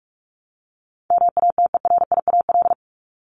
Morse code, heard as a keyed tone, about 44 wpm, 700 Hz. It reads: MATERIAL
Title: ES:S Suffix - Top 100 Words with Base Words 45wpm